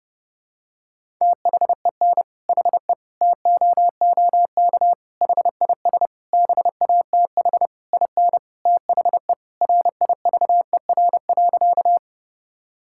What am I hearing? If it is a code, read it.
Morse code, heard as a keyed tone, about 32 wpm, 700 Hz. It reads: THEN HE TOOK HIS BATH IN THE RIVER.
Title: Sentences from Top 1600 Words 30wpm